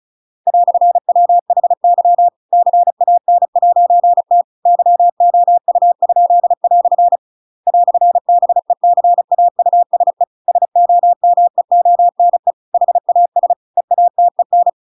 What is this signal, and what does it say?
Morse code, heard as a keyed tone, about 35 wpm, 700 Hz. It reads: "WHY CAN'T YOU?" "BECAUSE SOMEONE HAS EATEN
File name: Pinocchio / Pinocchio ch7 35 35 ik1hge